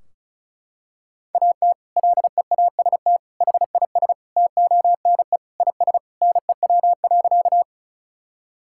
Morse code, heard as a keyed tone, about 36 wpm, 700 Hz. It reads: AT LEAST HIS TONE IS NEW.
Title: Sentences from Top 1600 Words 35wpm